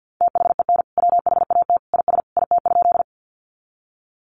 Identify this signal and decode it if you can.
Morse code, heard as a keyed tone, about 50 wpm, 700 Hz. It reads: THEN WHAT IS IT?
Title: Sentences from Top 100 Words 50wpm